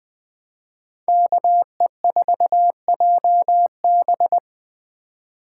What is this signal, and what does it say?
Morse code, heard as a keyed tone, about 20 wpm, 700 Hz. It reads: KE4JB
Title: Speed Racing - USA 2x2 Call Signs 17wpm